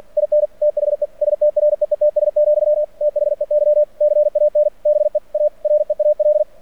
Morse code, heard as a keyed tone, about 46 wpm, 600 Hz. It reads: IN THE STREETS, THEY CAN BE A REAL
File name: archived / CRSnet